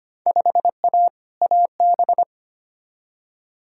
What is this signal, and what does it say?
Morse code, heard as a keyed tone, about 25 wpm, 700 Hz. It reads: 5A AB
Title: Field Day Exchanges Encore - 25wpm